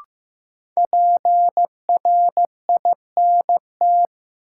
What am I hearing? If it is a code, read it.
Morse code, heard as a keyed tone, about 15 wpm, 700 Hz. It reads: PRINT
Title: Sentences from Top 1400 Words - New Word Drill Alt - 15wpm